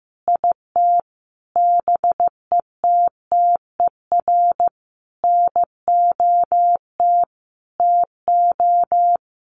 Morse code, heard as a keyed tone, about 14 wpm, 700 Hz. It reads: IT BETTER NOT TO
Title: Sentences from Top 600 Words 15wpm